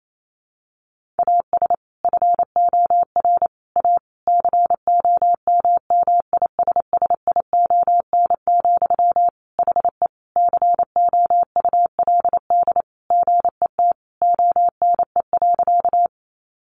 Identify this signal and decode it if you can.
Morse code, heard as a keyed tone, about 28 wpm, 700 Hz. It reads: AS FOR A COMMISSION, HE COULD GET ONE.
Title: Sentences from Top 1400 Words 28wpm